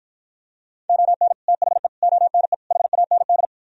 Morse code, heard as a keyed tone, about 52 wpm, 700 Hz. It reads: ON THE ONE HAND